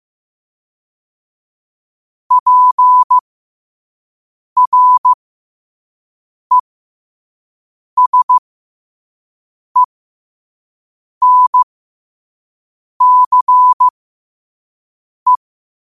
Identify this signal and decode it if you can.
Morse code, heard as a keyed tone, about 15 wpm, spaced as for 5 wpm, 1000 Hz. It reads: PRESENCE
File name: 5wpm